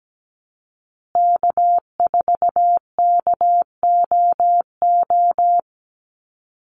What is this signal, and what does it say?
Morse code, heard as a keyed tone, about 17 wpm, 700 Hz. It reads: K4KOO
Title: USA Call Signs Repeated 2x 17wpm